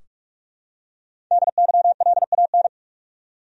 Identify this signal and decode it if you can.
Morse code, heard as a keyed tone, about 42 wpm, 700 Hz. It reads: DYLAN
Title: USA Common Names - Max 5 Letters - 45wpm